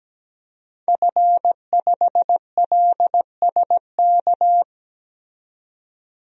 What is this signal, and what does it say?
Morse code, heard as a keyed tone, about 17 wpm, 700 Hz. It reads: F5LSK
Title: DXCC-100 Call Signs - 5 Characters - 17wpm